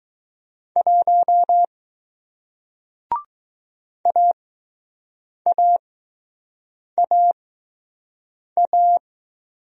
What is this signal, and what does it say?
Morse code, heard as a keyed tone, about 20 wpm, spaced as for 6 wpm, 700 Hz. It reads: 1 AAAA